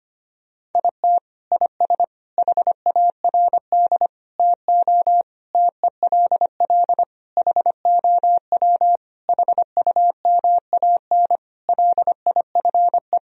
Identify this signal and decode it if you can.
Morse code, heard as a keyed tone, about 25 wpm, 700 Hz. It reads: IT IS HARD TO TELL HOW HUMAN LIFE
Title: Sentences from Top 1000 Words - Review Part 3 - 25wpm